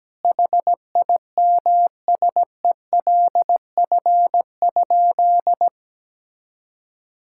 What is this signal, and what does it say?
Morse code, heard as a keyed tone, about 17 wpm, 700 Hz. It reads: HIMSELF?
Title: Sentences from Top 1600 Words 17wpm